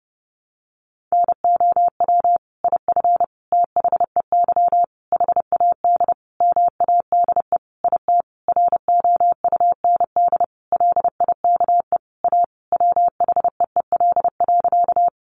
Morse code, heard as a keyed tone, about 30 wpm, 700 Hz. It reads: NOW IF THEY HAD MADE IT ROUND LIKE A WHEEL.
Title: Sentences from Top 500 Words 30wpm